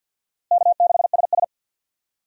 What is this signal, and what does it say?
Morse code, heard as a keyed tone, about 52 wpm, 700 Hz. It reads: K6SS
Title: USA Call Signs - 4 Characters - Repeated 2x - 50wpm